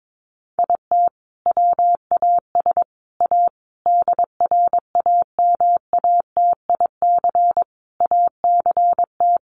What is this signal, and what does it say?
Morse code, heard as a keyed tone, about 22 wpm, 700 Hz. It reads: IT WAS A DRAMATIC ACT
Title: Sentences from Top 1300 Words 22wpm